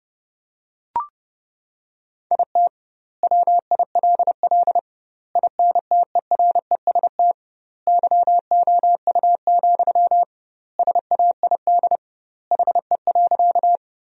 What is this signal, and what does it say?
Morse code, heard as a keyed tone, about 30 wpm, 700 Hz. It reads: IT WILL INTEREST YOU, SAID HE.
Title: 2x Word-Spacing - Sentences from Top 400 Words - 30wpm